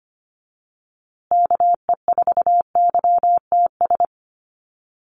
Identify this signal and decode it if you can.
Morse code, heard as a keyed tone, about 25 wpm, 700 Hz. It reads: KE4YTS
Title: USA Call Signs Repeated 3x 25wpm